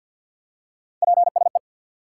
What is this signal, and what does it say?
Morse code, heard as a keyed tone, about 47 wpm, 700 Hz. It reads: PSE
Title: Top 90 QSO Elements 50wpm